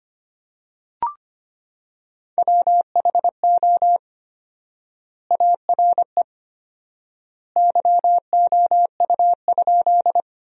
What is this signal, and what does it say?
Morse code, heard as a keyed tone, about 25 wpm, 700 Hz. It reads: WHO ARE YOU?
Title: 4x Word-Spacing - Sentences from Top 100 Words - 25wpm